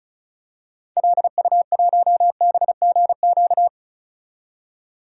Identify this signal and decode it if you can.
Morse code, heard as a keyed tone, about 35 wpm, 700 Hz. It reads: LU1BGQ
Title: DXCC-100 Call Signs - 6 Characters - 35wpm